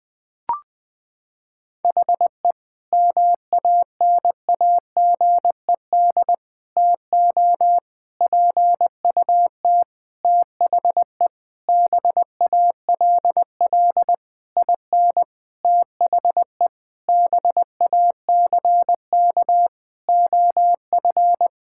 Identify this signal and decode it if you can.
Morse code, heard as a keyed tone, about 20 wpm, 700 Hz. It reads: HE MANAGED TO PUT THE BALL IN THE BACK OF
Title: Sentences with 9-Word Phrases - 20wpm